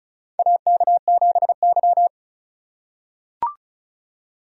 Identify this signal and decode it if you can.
Morse code, heard as a keyed tone, about 34 wpm, 700 Hz. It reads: AK7Y
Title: USA Call Signs - 4 Characters - 2x Character Spacing - 35wpm